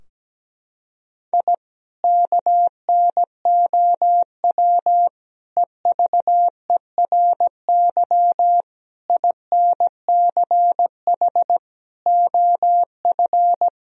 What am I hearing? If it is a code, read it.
Morse code, heard as a keyed tone, about 17 wpm, 700 Hz. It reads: I KNOW EVERY INCH OF
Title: Sentences from Top 1000 Words - Review Part 3 - 17wpm